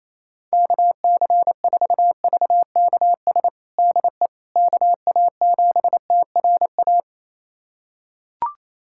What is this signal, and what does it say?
Morse code, heard as a keyed tone, about 28 wpm, 700 Hz. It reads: KC4VKS DE KA7TRA
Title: USA Callsign DE Callsign 28wpm